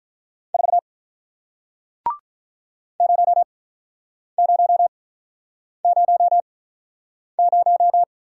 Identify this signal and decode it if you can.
Morse code, heard as a keyed tone, about 48 wpm, spaced as for 9 wpm, 700 Hz. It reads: 4 0000